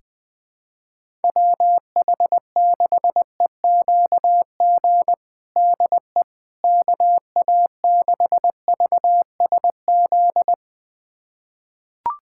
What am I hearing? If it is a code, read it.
Morse code, heard as a keyed tone, about 20 wpm, 700 Hz. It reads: WH6EQG DE KA6VSZ
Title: USA Callsign DE Callsign - 6 Characters - 20wpm